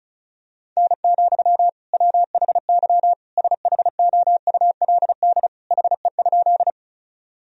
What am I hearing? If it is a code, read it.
Morse code, heard as a keyed tone, about 36 wpm, 700 Hz. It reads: N, WHY SHOULD HE?
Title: Sentences from Top 300 Words and Abbreviations 35wpm